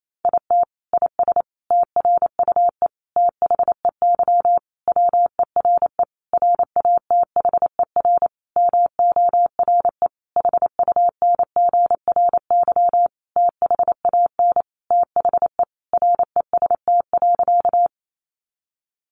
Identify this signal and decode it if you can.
Morse code, heard as a keyed tone, about 28 wpm, 700 Hz. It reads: IT IS TRUE THEY WERE RATHER MORE HUNGRY THAN THE REST.
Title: Sentences from Top 1400 Words 28wpm